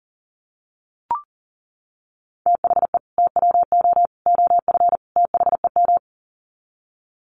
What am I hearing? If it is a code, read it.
Morse code, heard as a keyed tone, about 40 wpm, 700 Hz. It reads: THE TWO OF THEM
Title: Sets of 4 Words 40wpm